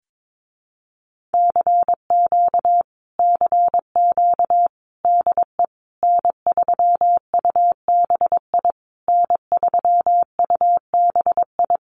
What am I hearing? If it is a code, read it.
Morse code, heard as a keyed tone, about 22 wpm, 700 Hz. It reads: CQ CQ DE N3UBI N3UBI
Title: Common QSO Phrases Encore 22wpm